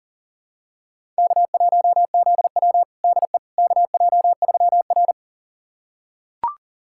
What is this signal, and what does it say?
Morse code, heard as a keyed tone, about 37 wpm, 700 Hz. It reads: K1ZW DE KJ3R
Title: USA Callsign DE Callsign - 4 Characters - 40wpm